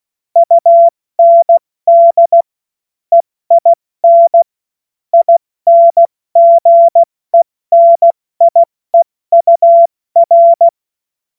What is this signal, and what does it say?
Morse code, heard as a keyed tone, about 16 wpm, spaced as for 14 wpm, 680 Hz. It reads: UND EIN INGENIEUR
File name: Morsekurs / Witze / Witz 21. Dezember 2021, 70 F80